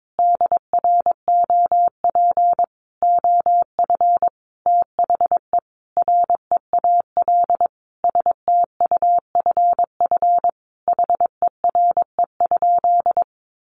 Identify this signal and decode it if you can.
Morse code, heard as a keyed tone, about 22 wpm, 700 Hz. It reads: DROP OF THE REAL STUFF HERE?
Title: Sentences from Top 1400 Words 22wpm